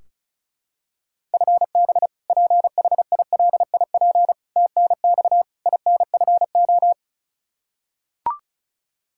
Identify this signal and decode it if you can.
Morse code, heard as a keyed tone, about 36 wpm, 700 Hz. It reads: FB PHILIP TNX INFO
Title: Practice QSO - Exchange 3 - More Info - 35wpm